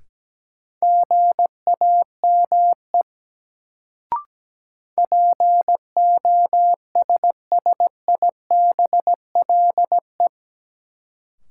Morse code, heard as a keyed tone, about 17 wpm, 700 Hz. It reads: GAME POSSIBLE
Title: Sentences from Top 500 Words - New Word Drill - 17wpm